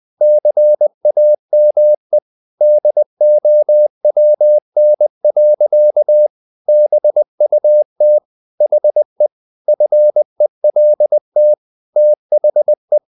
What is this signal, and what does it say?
Morse code, heard as a keyed tone, about 20 wpm, 600 Hz. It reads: CAME DOWN. BUT HE FELT THE
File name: the old man and the sea part 2 20wpm